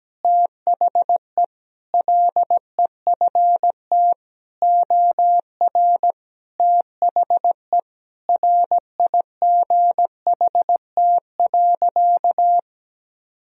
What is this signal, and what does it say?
Morse code, heard as a keyed tone, about 17 wpm, 700 Hz. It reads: THE LEFT OR THE RIGHT.
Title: Sentences from Top 1000 Words - Review Part 4 - 17wpm